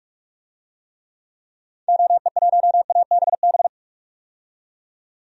Morse code, heard as a keyed tone, about 46 wpm, 700 Hz. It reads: OE1ABB
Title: DXCC-100 Call Signs - 6 Characters - 45wpm